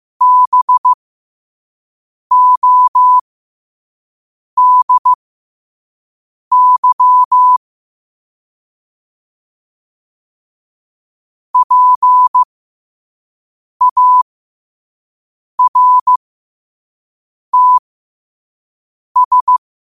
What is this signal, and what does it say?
Morse code, heard as a keyed tone, about 15 wpm, spaced as for 5 wpm, 1000 Hz. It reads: BODY PARTS
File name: 5wpm